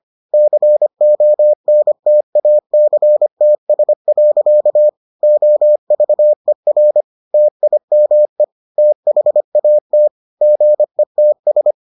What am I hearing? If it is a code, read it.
Morse code, heard as a keyed tone, about 25 wpm, 600 Hz. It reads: CONTACTS. OVER TIME THAT GETS